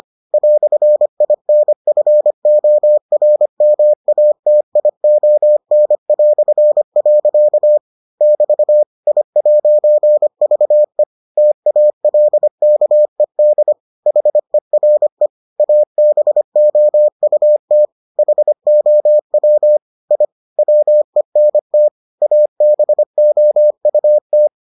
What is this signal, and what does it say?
Morse code, heard as a keyed tone, about 25 wpm, 600 Hz. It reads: "INFORMATION". <BT> I'VE TALKED HERE ABOUT HOW I WENT ABOUT